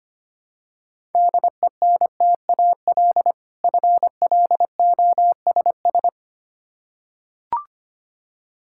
Morse code, heard as a keyed tone, about 25 wpm, 700 Hz. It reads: DENTAL FLOSS
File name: practice / 25wpm / Difficult Words with Hints - 25wpm